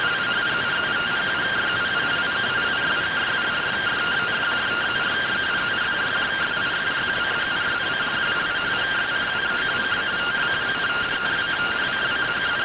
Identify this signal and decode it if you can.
Two-tone teleprinter signal, about 170 Hz shift.